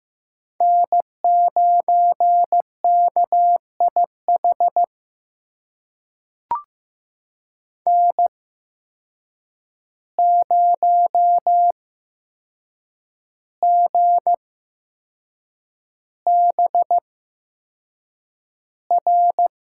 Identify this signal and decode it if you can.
Morse code, heard as a keyed tone, about 15 wpm, spaced as for 6 wpm, 700 Hz. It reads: N9KIH N0GBR
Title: USA Call Signs - 5 Characters - 8x Character Spacing - 15wpm